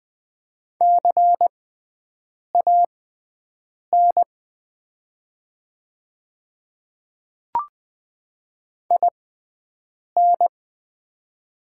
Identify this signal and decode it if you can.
Morse code, heard as a keyed tone, about 20 wpm, spaced as for 7 wpm, 700 Hz. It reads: CAN IN
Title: 6x Character Spacing - Top 500 Words - 20wpm